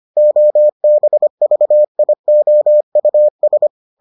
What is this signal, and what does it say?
Morse code, heard as a keyed tone, about 25 wpm, 600 Hz. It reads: OBVIOUS